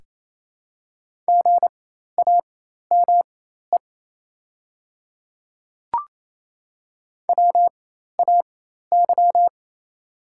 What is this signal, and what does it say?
Morse code, heard as a keyed tone, about 28 wpm, spaced as for 13 wpm, 700 Hz. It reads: GAME WAY
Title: 4x Character Spacing - Top 500 Words - 28wpm